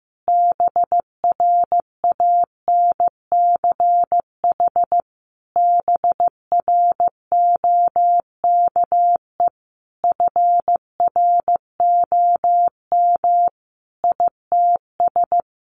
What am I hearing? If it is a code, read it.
Morse code, heard as a keyed tone, about 15 wpm, 700 Hz. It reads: BRANCH BROKE FROM ITS
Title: A Sound Of Thunder 15wpm